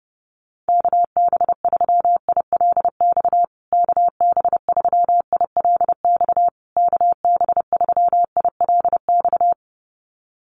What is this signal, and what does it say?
Morse code, heard as a keyed tone, about 28 wpm, 700 Hz. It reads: KB3ILX KB3ILX KB3ILX
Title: USA Call Signs Repeated 3x 30wpm